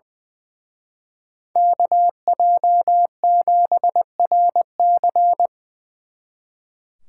Morse code, heard as a keyed tone, about 24 wpm, 700 Hz. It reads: KJ7RC
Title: Speed Racing - USA Call Signs 20wpm